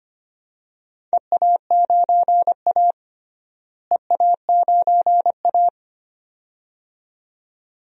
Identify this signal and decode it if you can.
Morse code, heard as a keyed tone, about 25 wpm, 700 Hz. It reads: EA9A EA9A
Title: DXCC-100 Call Signs - 4 Characters - Repeated 2x - 25wpm